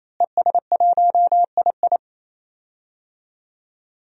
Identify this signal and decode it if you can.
Morse code, heard as a keyed tone, about 28 wpm, 700 Hz. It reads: ES1II
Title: DXCC-100 Call Signs - 5 Characters - Repeated 3x - 28wpm